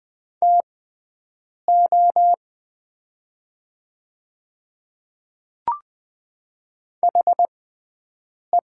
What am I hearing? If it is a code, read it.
Morse code, heard as a keyed tone, about 20 wpm, spaced as for 7 wpm, 700 Hz. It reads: TO HE
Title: 6x Character Spacing - Top 100 Words - 20wpm